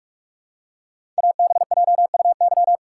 Morse code, heard as a keyed tone, about 44 wpm, 700 Hz. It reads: ABJUY
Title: Code Groups - 5 Letters - 45wpm